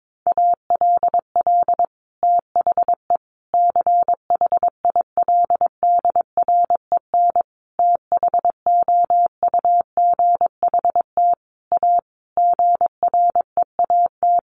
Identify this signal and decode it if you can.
Morse code, heard as a keyed tone, about 22 wpm, 700 Hz. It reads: ALL THE CHILDREN THOUGHT A GREAT
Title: Sentences from Top 800 Words 22wpm